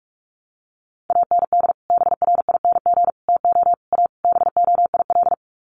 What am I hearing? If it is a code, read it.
Morse code, heard as a keyed tone, about 44 wpm, 700 Hz. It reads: AND BRING TO A BOIL
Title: Sets of 5 Words Encore 45wpm